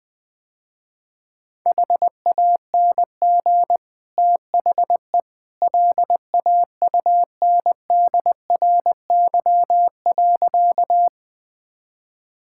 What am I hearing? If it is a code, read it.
Morse code, heard as a keyed tone, about 20 wpm, 700 Hz. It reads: HANG THE LAUNDRY.
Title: Easy Sentences with Hints - 20wpm